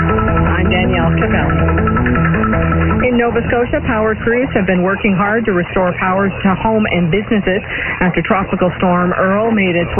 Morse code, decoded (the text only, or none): none